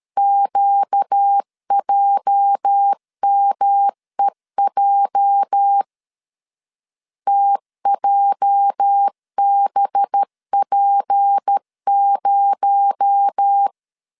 Morse code, heard as a keyed tone, about 13 wpm, 800 Hz. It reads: QJMEJ TJBP0
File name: LEZIONI / 8 YJ45 / 8 YJ 12wpm 2235